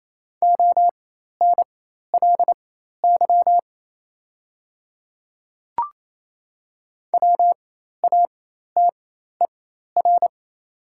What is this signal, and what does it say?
Morse code, heard as a keyed tone, about 27 wpm, spaced as for 13 wpm, 690 Hz. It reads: ONLY WATER
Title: 4x Character Spacing - Top 200 Words - 28wpm